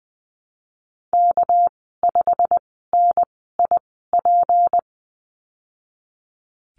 Morse code, heard as a keyed tone, about 20 wpm, spaced as for 15 wpm, 700 Hz. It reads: K5NIP
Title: 200 Lesson 32g - Letter k - Call Signs 20wpm